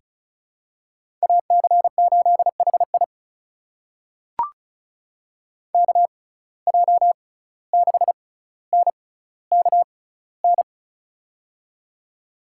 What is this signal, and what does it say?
Morse code, heard as a keyed tone, about 36 wpm, spaced as for 17 wpm, 700 Hz. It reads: AC8HI KJ6NKN
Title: USA Call Signs - 6x Character Spacing - 35wpm